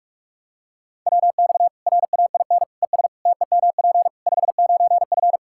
Morse code, heard as a keyed tone, about 42 wpm, 700 Hz. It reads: WX RAIN ES TEMP 59F
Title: Practice QSO - Exchange 2 - Info - 45wpm